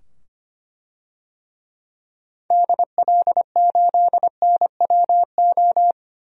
Morse code, heard as a keyed tone, about 25 wpm, 700 Hz. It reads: DL8NWO
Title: DXCC-100 Call Signs - 6 Characters - 25wpm